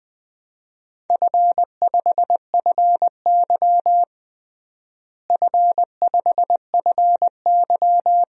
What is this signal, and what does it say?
Morse code, heard as a keyed tone, about 20 wpm, 700 Hz. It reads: F5FY F5FY
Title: DXCC-100 Call Signs - 4 Characters - Repeated 2x - 20wpm